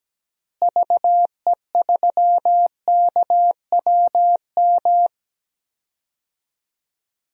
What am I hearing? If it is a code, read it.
Morse code, heard as a keyed tone, about 17 wpm, 700 Hz. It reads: VE3KWM
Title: DXCC-100 Call Signs - 6 Characters - Repeated 3x - 17wpm